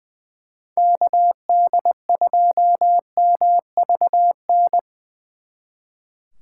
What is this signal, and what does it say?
Morse code, heard as a keyed tone, about 24 wpm, 700 Hz. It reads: KD2MVN
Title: Speed Racing - USA 2x3 Call Signs 20wpm